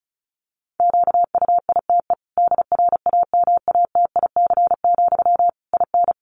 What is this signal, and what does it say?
Morse code, heard as a keyed tone, about 34 wpm, 700 Hz. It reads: QUITE DRAMATIC, IN
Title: Sentences from Top 1300 Words 35wpm